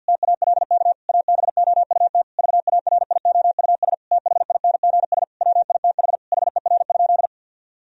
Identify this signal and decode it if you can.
Morse code, heard as a keyed tone, about 50 wpm, 700 Hz. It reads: TALK ABOUT VARIOUS THINGS WITH HER?
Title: Sentences from Top 1000 Words - Review Part 4 - 50wpm